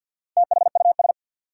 Morse code, heard as a keyed tone, about 48 wpm, 700 Hz. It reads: THUS
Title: Sentences from Top 800 Words - New Word Drill - 50wpm